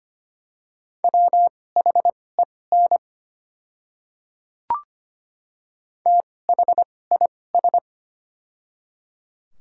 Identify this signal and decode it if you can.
Morse code, heard as a keyed tone, about 25 wpm, spaced as for 18 wpm, 690 Hz. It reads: WHEN THIS
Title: 2x Character Spacing - Top 200 Words - 25wpm